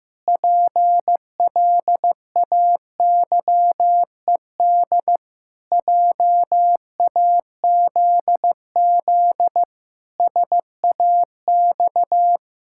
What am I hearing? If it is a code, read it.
Morse code, heard as a keyed tone, about 15 wpm, 700 Hz. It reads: PLAYED JAZZ SAX
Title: Pangram Challenge v2 - 15wpm